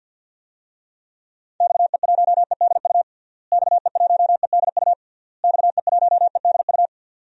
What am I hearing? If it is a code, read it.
Morse code, heard as a keyed tone, about 50 wpm, 700 Hz. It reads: XE1EDU XE1EDU XE1EDU
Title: DXCC-100 Call Signs - 6 Characters - Repeated 3x - 50wpm